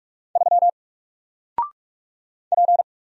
Morse code, heard as a keyed tone, about 44 wpm, 700 Hz. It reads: 3 P
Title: Single Letter-Number 45wpm